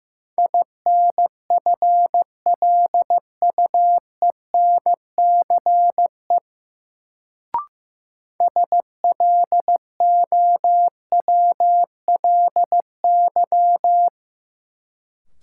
Morse code, heard as a keyed tone, about 15 wpm, 700 Hz. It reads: INFLUENCE SLOWLY
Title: Sentences from Top 1100 Words - New Word Drill Alt - 15wpm